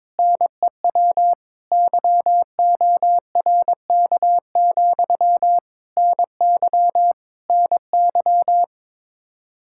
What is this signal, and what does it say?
Morse code, heard as a keyed tone, about 22 wpm, 700 Hz. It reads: NEW YORK, NY NY
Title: K1USN SST QSO Practice 22wpm